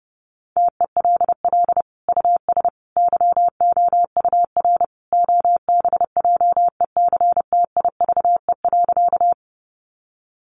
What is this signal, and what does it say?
Morse code, heard as a keyed tone, about 30 wpm, 700 Hz. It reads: TELL US YOUR OBJECTIVE.
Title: Sentences from Top 1200 Words 30wpm